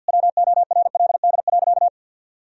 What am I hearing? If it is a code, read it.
Morse code, heard as a keyed tone, about 51 wpm, 700 Hz. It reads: WORLD.
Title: Sentences from Top 1000 Words - Review Part 3 - 50wpm